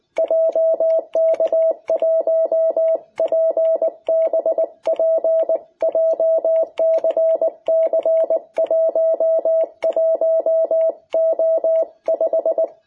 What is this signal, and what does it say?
Morse code, heard as a keyed tone, about 20 wpm, 660 Hz. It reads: JK1PBP JCC11O5
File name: jh8cbh / 2021cw